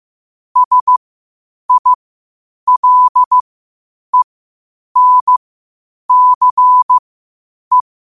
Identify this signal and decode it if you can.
Morse code, heard as a keyed tone, about 15 wpm, spaced as for 8 wpm, 1000 Hz. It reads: SILENCE
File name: qotdfiles / 8wpm